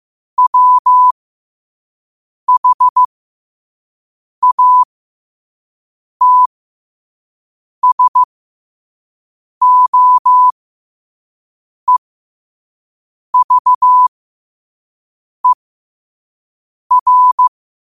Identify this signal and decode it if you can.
Morse code, heard as a keyed tone, about 15 wpm, spaced as for 5 wpm, 1000 Hz. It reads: WHATSOEVER
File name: qotdfiles / 5wpm